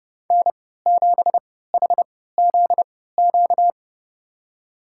Morse code, heard as a keyed tone, about 32 wpm, spaced as for 17 wpm, 700 Hz. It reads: N7HZQ
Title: USA Call Signs - 5 Characters - 3x Character Spacing - 30wpm